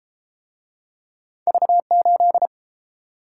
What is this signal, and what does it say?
Morse code, heard as a keyed tone, about 32 wpm, 700 Hz. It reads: V8